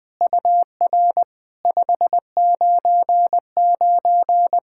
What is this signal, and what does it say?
Morse code, heard as a keyed tone, about 20 wpm, 700 Hz. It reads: UR 599
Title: SOTA QSO Practice 20wpm